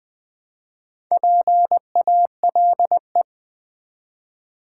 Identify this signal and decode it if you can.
Morse code, heard as a keyed tone, about 20 wpm, 700 Hz. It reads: PALE
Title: Build A Word - 1 Letter At A Time - 3 to 5-Letter Words - 20wpm